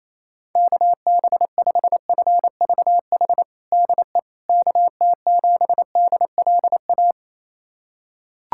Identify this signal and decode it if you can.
Morse code, heard as a keyed tone, about 28 wpm, 700 Hz. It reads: KB5FVH DE KT7DLA
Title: USA Callsign DE Callsign 28wpm